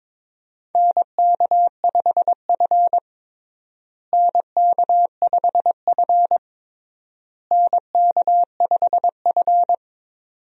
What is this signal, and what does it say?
Morse code, heard as a keyed tone, about 22 wpm, 700 Hz. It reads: NK5F NK5F NK5F
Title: USA Call Signs - 4 Characters - Repeated 3x - 22wpm